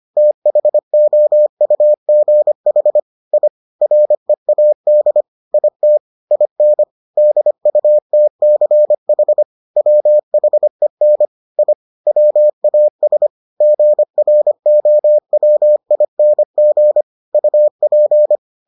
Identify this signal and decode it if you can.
Morse code, heard as a keyed tone, about 25 wpm, 600 Hz. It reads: THOUGH I READ IT IN DUTCH WHEN I WAS GROWING UP